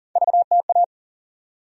Morse code, heard as a keyed tone, about 40 wpm, 700 Hz. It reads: VTA
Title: USA Call Sign Postfixes - 3 Letters 40wpm